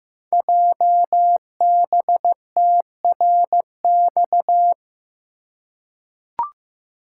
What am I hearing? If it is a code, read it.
Morse code, heard as a keyed tone, about 15 wpm, 700 Hz. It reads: JBTRX
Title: Code Groups - 5 Letters - 15wpm